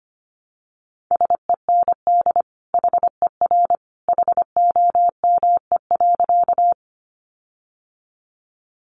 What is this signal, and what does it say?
Morse code, heard as a keyed tone, about 25 wpm, 700 Hz. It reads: SEND HER HOME.
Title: Sentences from Top 1500 Words 25wpm